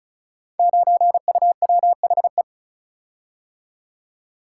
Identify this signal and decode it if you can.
Morse code, heard as a keyed tone, about 36 wpm, 690 Hz. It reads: 9UWHE
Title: Code Groups - 5 Letter:Numbers - 35wpm